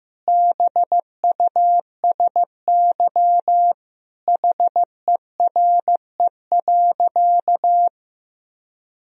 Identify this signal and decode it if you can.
Morse code, heard as a keyed tone, about 16 wpm, 700 Hz. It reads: BUSY HERE.
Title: Sentences from Top 1000 Words - Review Part 3 - 15wpm